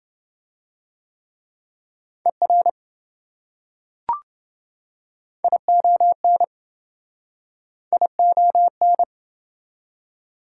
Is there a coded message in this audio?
Yes: Morse code from 2.3 to 9.0 s, heard as a keyed tone, about 27 wpm, 700 Hz. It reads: ER ION ION